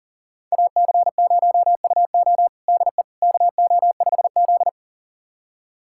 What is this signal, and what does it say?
Morse code, heard as a keyed tone, about 40 wpm, 700 Hz. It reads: AC0UO DE KO5Z
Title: USA Callsign DE Callsign 40wpm